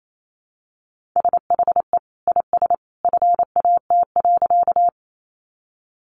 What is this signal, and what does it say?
Morse code, heard as a keyed tone, about 28 wpm, 700 Hz. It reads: SHE IS FAT.
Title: Sentences from Top 600 Words 28wpm